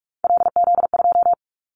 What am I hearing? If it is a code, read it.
Morse code, heard as a keyed tone, about 46 wpm, 700 Hz. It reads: LZ2
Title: Callsign Prefixes for Top 100 DXCC - 45wpm